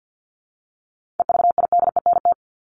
Morse code, heard as a keyed tone, about 50 wpm, 700 Hz. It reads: EVIDENT
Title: LY Suffix - Top 100 Words with Base Words 50wpm